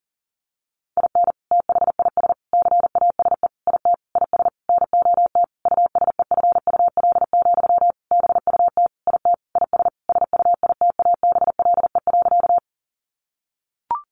Morse code, heard as a keyed tone, about 40 wpm, 700 Hz. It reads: IN THIS CASE IT IS NOT USEFUL, BUT IT IS SUITABLE.
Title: Sentences from Top 1400 Words 40wpm